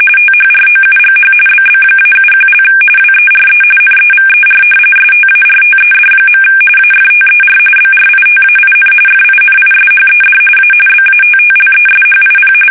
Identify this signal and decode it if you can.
RTTY, about 45.45 baud. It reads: THROUGH BB-71) WERE NEVER STARTED. AFTER BRITAIN FINISHED A BATTLESHIP
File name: fsk850 / battleshipnj